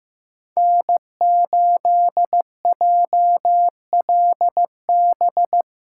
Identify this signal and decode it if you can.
Morse code, heard as a keyed tone, about 18 wpm, 700 Hz. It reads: N8JLB
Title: Speed Racing - USA 1x3 Call Signs 15wpm